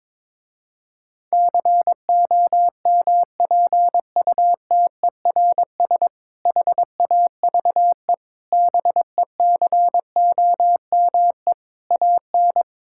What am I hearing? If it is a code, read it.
Morse code, heard as a keyed tone, about 22 wpm, 700 Hz. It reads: COMPUTERS HAVE BECOME AN
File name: Difficult Sentences with Hints - 22wpm